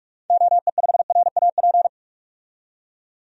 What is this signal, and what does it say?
Morse code, heard as a keyed tone, about 42 wpm, 700 Hz. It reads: OE5RAP
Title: DXCC-100 Call Signs - 6 Characters - 45wpm